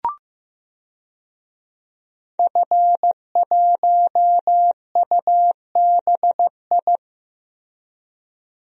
Morse code, heard as a keyed tone, about 15 wpm, 700 Hz. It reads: F1UBI
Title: DXCC-100 Call Signs - 5 Characters - Repeated 3x - 15wpm